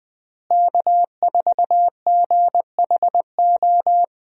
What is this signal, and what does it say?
Morse code, heard as a keyed tone, about 20 wpm, 700 Hz. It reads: K4GHO
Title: USA Call Signs - 5 Characters - Repeated 2x - 20wpm